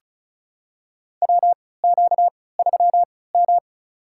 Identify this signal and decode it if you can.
Morse code, heard as a keyed tone, about 36 wpm, spaced as for 20 wpm, 700 Hz. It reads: WQ3M
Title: USA Call Signs - 4 Characters - 3x Character Spacing - 35wpm